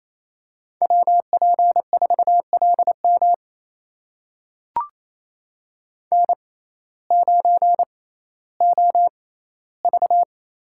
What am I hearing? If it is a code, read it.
Morse code, heard as a keyed tone, about 28 wpm, spaced as for 15 wpm, 700 Hz. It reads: WP4LM N9OV
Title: USA Call Signs - 5 Characters - 6x Character Spacing - 28wpm